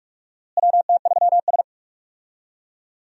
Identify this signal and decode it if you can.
Morse code, heard as a keyed tone, about 46 wpm, 700 Hz. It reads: WT3S
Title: USA Call Signs - 4 Characters - Repeated 3x - 45wpm